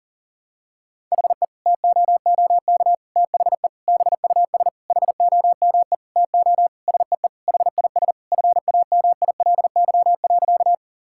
Morse code, heard as a keyed tone, about 40 wpm, 700 Hz. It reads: HE TOOK THE BUS HOME TO SEE HIS FAMILY.
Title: Sentences from Top 1000 Words - Review Part 6 - 40wpm